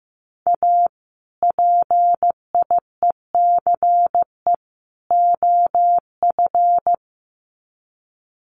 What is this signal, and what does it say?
Morse code, heard as a keyed tone, about 15 wpm, 700 Hz. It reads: A PIECE OF
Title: Sets of 3 Words 15wpm